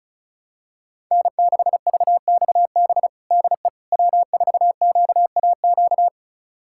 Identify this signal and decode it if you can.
Morse code, heard as a keyed tone, about 36 wpm, 700 Hz. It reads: N6VXB DE W4QAQ
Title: USA Callsign DE Callsign - 5 Characters - 35wpm